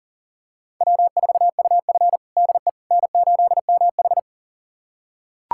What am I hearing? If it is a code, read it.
Morse code, heard as a keyed tone, about 40 wpm, 700 Hz. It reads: W4UF DE N8MH
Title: USA Callsign DE Callsign - 4 Characters - 40wpm